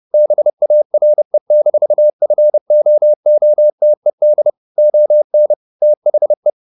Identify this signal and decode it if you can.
Morse code, heard as a keyed tone, about 30 wpm, 600 Hz. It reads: BARE-FOOTED ON THE
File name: the old man and the sea / the old man and the sea 1 30wpm